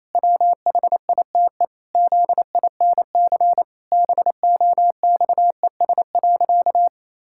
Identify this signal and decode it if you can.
Morse code, heard as a keyed tone, about 28 wpm, 700 Hz. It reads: WHITE ZINC BOXES.
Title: Pangram Challenge v2 - 28wpm